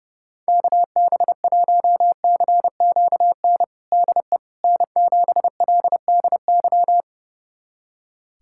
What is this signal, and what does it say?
Morse code, heard as a keyed tone, about 30 wpm, 700 Hz. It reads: KB1CQN DE N7LDY
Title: USA Callsign DE Callsign 30wpm